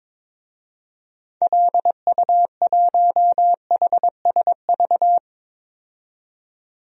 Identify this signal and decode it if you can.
Morse code, heard as a keyed tone, about 22 wpm, 700 Hz. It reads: LU1HSV
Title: DXCC-100 Call Signs - 6 Characters - 22wpm